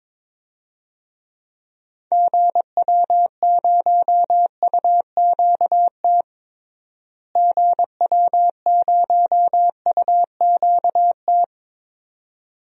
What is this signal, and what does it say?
Morse code, heard as a keyed tone, about 22 wpm, 700 Hz. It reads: GW0UQT GW0UQT
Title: DXCC-100 Call Signs - 6 Characters - Repeated 2x - 22wpm